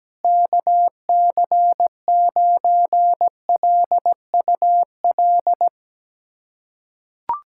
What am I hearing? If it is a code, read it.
Morse code, heard as a keyed tone, about 17 wpm, 700 Hz. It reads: KC9LUL
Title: USA Callsign DE Callsign - 6 Characters - 17wpm